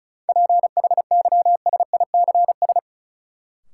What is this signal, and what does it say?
Morse code, heard as a keyed tone, about 36 wpm, 700 Hz. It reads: PHYSICS
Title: Sentences from Top 1000 Words - Review Part 5 - New Word Drill - 35wpm